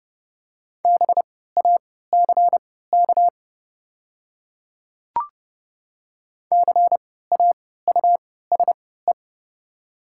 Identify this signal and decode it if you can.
Morse code, heard as a keyed tone, about 30 wpm, spaced as for 17 wpm, 700 Hz. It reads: BACK CAUSE
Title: 3x Character Spacing - Top 200 Words - 30wpm